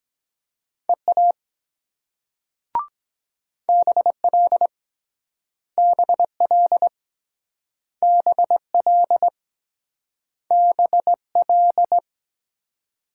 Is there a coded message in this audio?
Yes: Morse code from 0.9 to 12.0 s, heard as a keyed tone, about 23 wpm, 700 Hz. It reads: EA BL BL BL BL